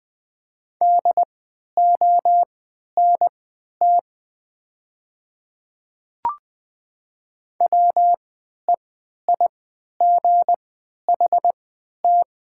Morse code, heard as a keyed tone, about 20 wpm, spaced as for 11 wpm, 700 Hz. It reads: DONT WEIGHT
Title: 3x Character Spacing - Top 500 Words - 20wpm